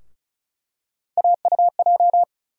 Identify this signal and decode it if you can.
Morse code, heard as a keyed tone, about 36 wpm, 700 Hz. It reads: AUJ
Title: USA Call Sign Postfixes - 3 Letters 35wpm